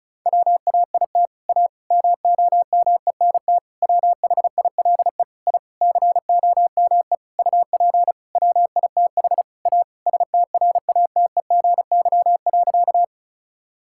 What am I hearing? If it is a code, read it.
Morse code, heard as a keyed tone, about 36 wpm, 700 Hz. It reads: WAIT A MOMENT WHILE I COME UP WITH A STRATEGY.
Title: Sentences from Top 1000 Words - Review Part 5 - 35wpm